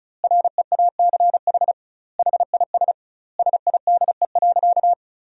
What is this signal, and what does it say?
Morse code, heard as a keyed tone, about 34 wpm, 700 Hz. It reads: REACH HIS SIDE.
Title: 2x Word-Spacing - Sentences from Top 400 Words - 35wpm